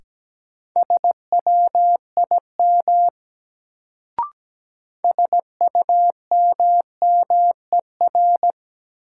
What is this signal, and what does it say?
Morse code, heard as a keyed tone, about 17 wpm, 700 Hz. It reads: SWIM SUMMER
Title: Sentences from Top 1000 Words - Review Part 6 - New Word Drill - 17wpm